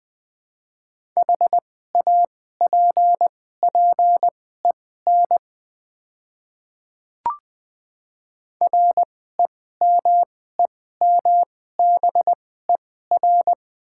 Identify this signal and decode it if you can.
Morse code, heard as a keyed tone, about 20 wpm, spaced as for 14 wpm, 700 Hz. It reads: HAPPEN REMEMBER
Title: 2x Character Spacing - Top 500 Words - 20wpm